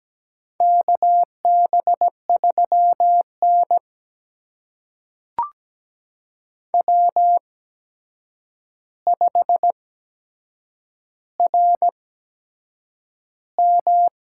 Morse code, heard as a keyed tone, about 17 wpm, spaced as for 7 wpm, 700 Hz. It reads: KB3N W5RM
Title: USA Call Signs - 4 Characters - 8x Character Spacing - 17wpm